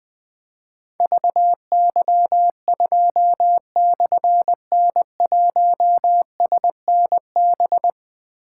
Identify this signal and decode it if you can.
Morse code, heard as a keyed tone, about 20 wpm, 700 Hz. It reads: VY2/N1SNB
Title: Difficult Call Signs 20wpm